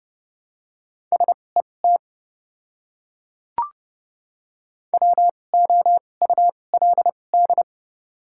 Morse code, heard as a keyed tone, about 30 wpm, spaced as for 22 wpm, 700 Hz. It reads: SET WOULD
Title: 2x Character Spacing - Top 300 Words - 30wpm